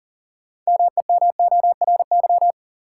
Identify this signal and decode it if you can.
Morse code, heard as a keyed tone, about 40 wpm, 700 Hz. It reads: MEMORY
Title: Sentences from Top 1300 Words - New Word Drill Alt - 40wpm